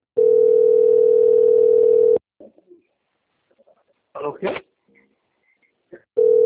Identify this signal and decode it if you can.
Ringback tone (North America), 440 + 480 Hz, with speech.
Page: Valinhos - SP